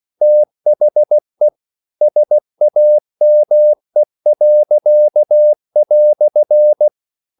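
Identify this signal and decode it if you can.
Morse code, heard as a keyed tone, about 16 wpm, 600 Hz. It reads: THE SAME."
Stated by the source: English quotations sent in CW as practice at 16 words per minute.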